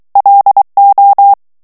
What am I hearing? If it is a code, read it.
Morse code, heard as a keyed tone, about 23 wpm, 800 Hz. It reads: LO
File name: ESPAGNOL TEXTE EN CLAIR 2 VITESSE 22 MOTS MINUTES